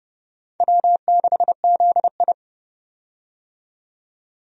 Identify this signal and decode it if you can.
Morse code, heard as a keyed tone, about 30 wpm, 700 Hz. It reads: W6ZI
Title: USA Call Signs - 4 Characters - 30wpm